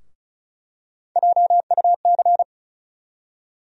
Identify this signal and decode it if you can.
Morse code, heard as a keyed tone, about 36 wpm, 700 Hz. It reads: JUC
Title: USA Call Sign Postfixes - 3 Letters 35wpm